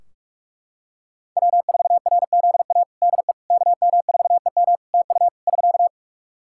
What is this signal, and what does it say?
Morse code, heard as a keyed tone, about 43 wpm, 700 Hz. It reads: W4RZA DE KM4EM TU <SK>
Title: Practice QSO - Exchange 4 - Ending - 45wpm